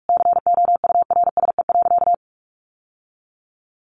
Morse code, heard as a keyed tone, about 44 wpm, 700 Hz. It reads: COURSE.
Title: Sentences from Top 1100 Words 45wpm